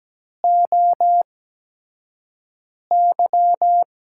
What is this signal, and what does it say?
Morse code, heard as a keyed tone, about 17 wpm, spaced as for 5 wpm, 700 Hz. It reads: OY
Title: USA Call Signs - 5 Characters - 8x Character Spacing - 17wpm